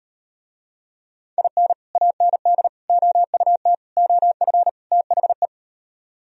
Morse code, heard as a keyed tone, about 38 wpm, 700 Hz. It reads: IN AND OUT OF THE